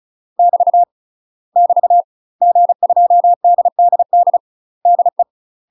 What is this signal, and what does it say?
Morse code, heard as a keyed tone, about 34 wpm, 700 Hz. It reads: <BT> <BT> G2DDD DE
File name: UK MASTER QSOs TRAINING / 35WPM / UK MASTER QSOs 0000